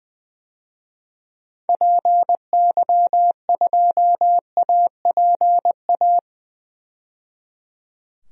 Morse code, heard as a keyed tone, about 20 wpm, 700 Hz. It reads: PY2APA
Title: DXCC-100 Call Signs - 6 Characters - 20wpm